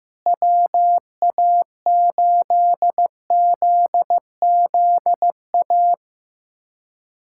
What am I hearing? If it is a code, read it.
Morse code, heard as a keyed tone, about 14 wpm, 700 Hz. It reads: WA8ZZA
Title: USA Callsign DE Callsign - 6 Characters - 15wpm